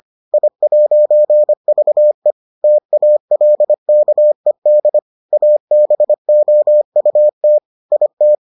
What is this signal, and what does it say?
Morse code, heard as a keyed tone, about 25 wpm, 600 Hz. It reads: I'VE TALKED ABOUT IT